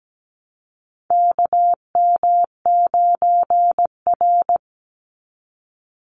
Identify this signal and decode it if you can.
Morse code, heard as a keyed tone, about 17 wpm, 700 Hz. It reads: KM9R
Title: Call Signs - 17wpm